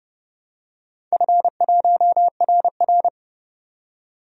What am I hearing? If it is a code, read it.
Morse code, heard as a keyed tone, about 30 wpm, 700 Hz. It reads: F1RR
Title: DXCC-100 Call Signs - 4 Characters - 30wpm